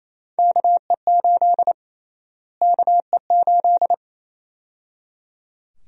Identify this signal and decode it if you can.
Morse code, heard as a keyed tone, about 28 wpm, 700 Hz. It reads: KE8 KE8
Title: USA Call Sign Prefixes - Repeated 2x - 28wpm